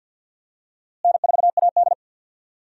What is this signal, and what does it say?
Morse code, heard as a keyed tone, about 49 wpm, 700 Hz. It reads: N4AD
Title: USA Call Signs - 4 Characters - Repeated 3x - 50wpm